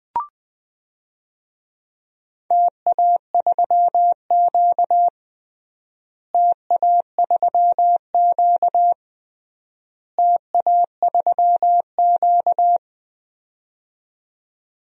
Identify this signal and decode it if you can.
Morse code, heard as a keyed tone, about 20 wpm, 700 Hz. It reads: TA3Q TA3Q TA3Q
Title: DXCC-100 Call Signs - 4 Characters - Repeated 3x - 20wpm